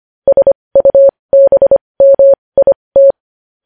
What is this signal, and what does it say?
Morse code, heard as a keyed tone, about 25 wpm, spaced as for 20 wpm, 550 Hz. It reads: SUBMIT
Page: news headlines in morse code at 20 wpm